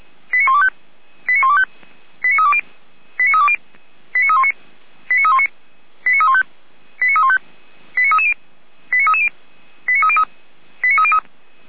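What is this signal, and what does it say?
Five-tone selective call (ZVEI), heard as quick runs of five tones, tones of about 70 ms. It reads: 89125 89125 89320 89320 89319 89319 89315 89315 89330 89330 89303 89303
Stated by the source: ZVEI five-tone selective call received on SDR.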